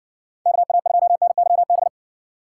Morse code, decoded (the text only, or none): KA2NYB